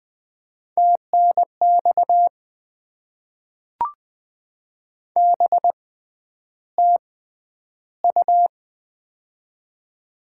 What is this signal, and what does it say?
Morse code, heard as a keyed tone, about 20 wpm, spaced as for 10 wpm, 690 Hz. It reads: TNX BTU
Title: Top 20 QSO Elements - 6x Character Spacing - 20wpm